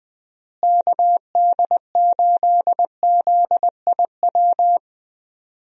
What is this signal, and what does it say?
Morse code, heard as a keyed tone, about 20 wpm, 700 Hz. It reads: KD8ZIW
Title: USA Call Signs - 6 Characters - Repeated 2x - 20wpm